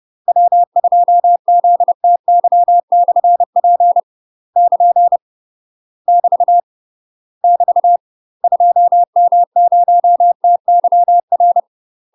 Morse code, heard as a keyed tone, about 30 wpm, 700 Hz. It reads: W2ZTY/P <KN> <BT> <BT> 2M0TYR
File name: UK MASTER QSOs TRAINING / 30WPM / UK MASTER QSOs 0045